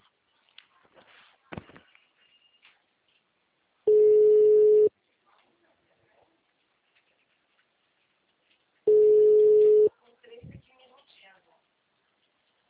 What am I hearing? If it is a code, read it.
Ringback tone (Europe, 425 Hz), with speech.
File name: out- -2008- - - .97675